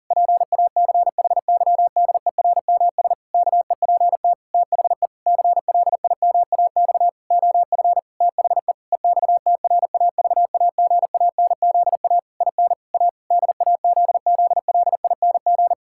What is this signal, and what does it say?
Morse code, heard as a keyed tone, about 37 wpm, 700 Hz. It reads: PACHYDERMS KEPT THE CLIMAX OF THE EXTRAVAGANZA IN A DAZZLING
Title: Pangram Challenge v2 - 40wpm